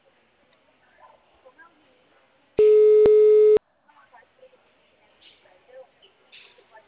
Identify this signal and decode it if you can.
Ringback tone (Europe, 425 Hz), with speech.
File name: out- -2010- - - .28758